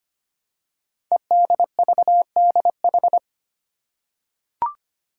Morse code, heard as a keyed tone, about 25 wpm, 700 Hz. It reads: EDVDH
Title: Code Groups - 5 Letters - 25wpm